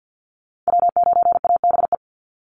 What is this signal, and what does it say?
Morse code, heard as a keyed tone, about 48 wpm, 700 Hz. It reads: W9ABE
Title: USA Call Signs 50wpm